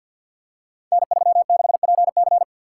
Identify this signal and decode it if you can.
Morse code, heard as a keyed tone, about 50 wpm, 700 Hz. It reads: N36PC
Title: Code Groups - 5 Letter:Numbers - 50wpm